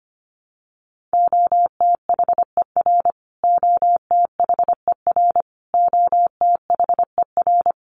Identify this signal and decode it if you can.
Morse code, heard as a keyed tone, about 25 wpm, 700 Hz. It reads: OTHER OTHER OTHER
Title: Top 300 Words Repeated 25wpm